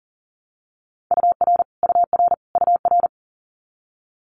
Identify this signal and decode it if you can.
Morse code, heard as a keyed tone, about 40 wpm, 700 Hz. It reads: UR UR UR
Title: Abbreviations Repeated 3x - 40wpm